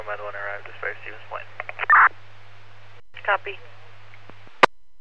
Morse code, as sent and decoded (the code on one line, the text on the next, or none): none